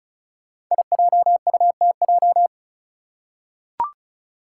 Morse code, heard as a keyed tone, about 34 wpm, 700 Hz. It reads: IJUTJ
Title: Five Challenging Letter-Number Combinations 35wpm